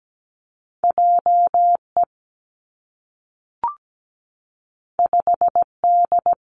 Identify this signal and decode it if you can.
Morse code, heard as a keyed tone, about 17 wpm, 700 Hz. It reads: JE 5D
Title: Two Challenging Letter-Number Combinations 17wpm